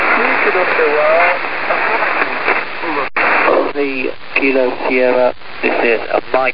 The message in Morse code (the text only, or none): none